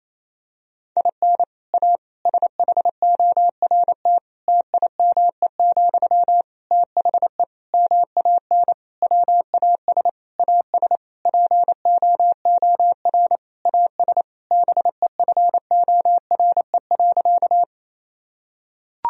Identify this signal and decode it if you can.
Morse code, heard as a keyed tone, about 28 wpm, 700 Hz. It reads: IN A SHORT TIME, THE MAN WAS AS POOR AS BEFORE.
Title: Sentences from Top 1000 Words - Review Part 2 - 28wpm